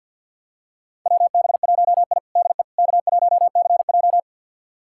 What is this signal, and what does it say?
Morse code, heard as a keyed tone, about 52 wpm, 700 Hz. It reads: WB1I DE K1CJ
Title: USA Callsign DE Callsign - 4 Characters - 50wpm